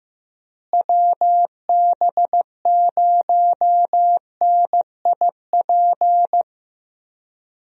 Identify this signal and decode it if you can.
Morse code, heard as a keyed tone, about 15 wpm, 700 Hz. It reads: WB0NIP
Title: USA Call Signs - 6 Characters - Repeated 2x - 15wpm